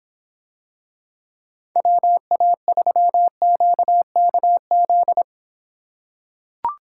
Morse code, heard as a keyed tone, about 26 wpm, 700 Hz. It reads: WA3QKZ